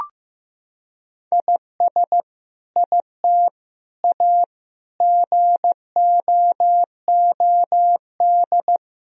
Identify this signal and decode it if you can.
Morse code, heard as a keyed tone, about 15 wpm, 700 Hz. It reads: IS IT A GOOD
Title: Sentences from Top 1000 Words - Review Part 4 - 15wpm